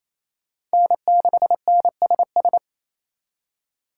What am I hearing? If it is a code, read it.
Morse code, heard as a keyed tone, about 36 wpm, 700 Hz. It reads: N6NSS